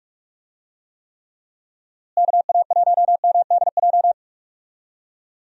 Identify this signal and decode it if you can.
Morse code, heard as a keyed tone, about 46 wpm, 700 Hz. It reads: KA1MDJ
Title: USA Call Signs - 6 Characters - Repeated 3x - 45wpm